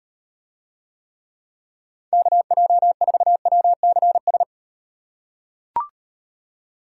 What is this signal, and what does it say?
Morse code, heard as a keyed tone, about 36 wpm, 700 Hz. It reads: KJ4WCS